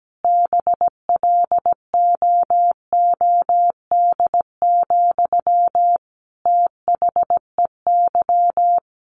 Morse code, heard as a keyed tone, about 17 wpm, 700 Hz. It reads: BLOOD, THEY
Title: Sentences from Top 1100 Words 17wpm